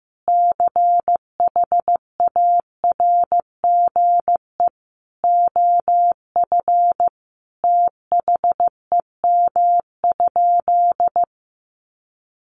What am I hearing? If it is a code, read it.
Morse code, heard as a keyed tone, about 15 wpm, 700 Hz. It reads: CHARGE OF THEM?
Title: Sentences from Top 900 Words 15wpm